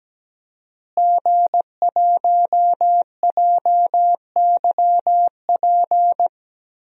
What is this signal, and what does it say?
Morse code, heard as a keyed tone, about 17 wpm, 700 Hz. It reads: G1JYP
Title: DXCC-100 Call Signs - 5 Characters - Repeated 3x - 17wpm